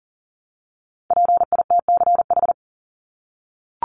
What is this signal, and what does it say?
Morse code, heard as a keyed tone, about 40 wpm, 700 Hz. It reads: PITCH
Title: Top 1251-1500 Words 40wpm